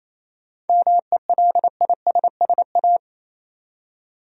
Morse code, heard as a keyed tone, about 28 wpm, 700 Hz. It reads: MELISSA
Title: USA Common Names - 28wpm